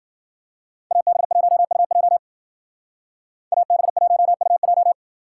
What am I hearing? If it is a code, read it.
Morse code, heard as a keyed tone, about 52 wpm, 700 Hz. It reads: AB1UJ AB1UJ